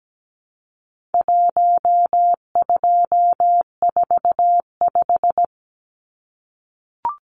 Morse code, heard as a keyed tone, about 17 wpm, 700 Hz. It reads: 1245
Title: Numbers - Contest Style - 4-Digit 17wpm